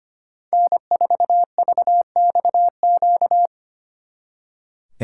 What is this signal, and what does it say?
Morse code, heard as a keyed tone, about 32 wpm, 700 Hz. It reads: N4VXQ